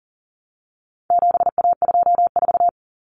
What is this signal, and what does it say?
Morse code, heard as a keyed tone, about 40 wpm, 700 Hz. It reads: 7A24
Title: Common Cut Numbers - 4-Digit 40wpm